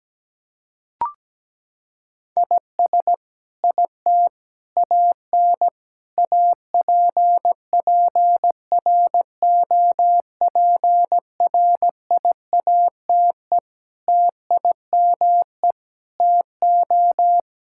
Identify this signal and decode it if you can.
Morse code, heard as a keyed tone, about 17 wpm, 700 Hz. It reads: IS IT AN APPROPRIATE TIME TO
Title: Sentences from Top 1000 Words - Review Part 5 - 17wpm